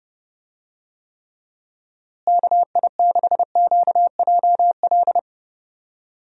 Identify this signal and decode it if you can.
Morse code, heard as a keyed tone, about 32 wpm, 700 Hz. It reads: KI6QJL
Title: USA Call Signs - 6 Characters - Repeated 2x - 30wpm